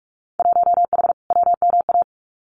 Morse code, heard as a keyed tone, about 44 wpm, 700 Hz. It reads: 1H WMA
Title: Winter Field Day Exchanges 45wpm